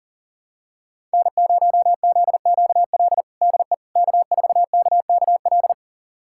Morse code, heard as a keyed tone, about 40 wpm, 700 Hz. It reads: N0ZQL DE K4KKL
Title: USA Callsign DE Callsign - 5 Characters - 40wpm